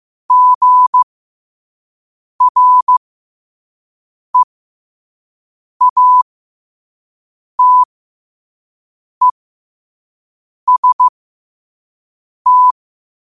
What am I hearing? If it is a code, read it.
Morse code, heard as a keyed tone, about 15 wpm, spaced as for 5 wpm, 1000 Hz. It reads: GREATEST
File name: qotdfiles / 5wpm